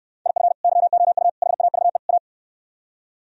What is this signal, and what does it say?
Morse code, heard as a keyed tone, about 67 wpm, 700 Hz. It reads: IF YOU HAVE A